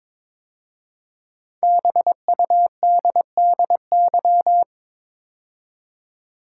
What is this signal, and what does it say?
Morse code, heard as a keyed tone, about 22 wpm, 700 Hz. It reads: BUDDY
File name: 5 Letter Words - First 3 Letters Given - 22wpm